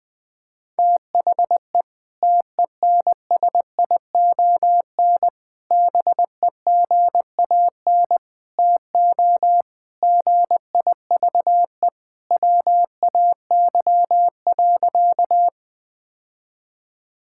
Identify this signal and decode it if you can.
Morse code, heard as a keyed tone, about 20 wpm, 700 Hz. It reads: THE TENSION BEGAN TO GIVE WAY.
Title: Sentences from Top 1600 Words 20wpm